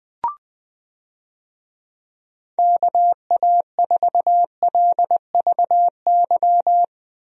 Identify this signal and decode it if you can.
Morse code, heard as a keyed tone, about 20 wpm, 700 Hz. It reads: KA4LVY
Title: USA Call Signs - 6 Characters - Repeated 2x - 20wpm